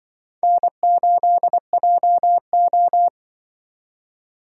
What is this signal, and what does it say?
Morse code, heard as a keyed tone, about 24 wpm, 700 Hz. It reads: N8JO
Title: Speed Racing - USA 1x2 Call Signs 20wpm